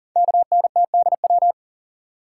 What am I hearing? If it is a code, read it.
Morse code, heard as a keyed tone, about 44 wpm, 700 Hz. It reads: KNTDW
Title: Code Groups - 5 Letters - 40wpm